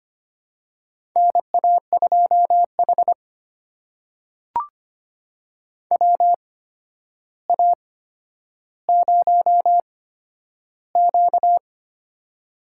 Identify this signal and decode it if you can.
Morse code, heard as a keyed tone, about 25 wpm, spaced as for 11 wpm, 700 Hz. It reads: NA2H WA0Q
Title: USA Call Signs - 4 Characters - 8x Character Spacing - 25wpm